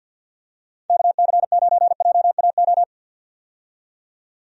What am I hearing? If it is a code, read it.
Morse code, heard as a keyed tone, about 51 wpm, 700 Hz. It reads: KC9JAO
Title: USA Call Signs - 6 Characters - 50wpm